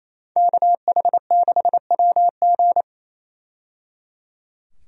Morse code, heard as a keyed tone, about 28 wpm, 700 Hz. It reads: KH6WG
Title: DXCC-100 Call Signs - 5 Characters - 28wpm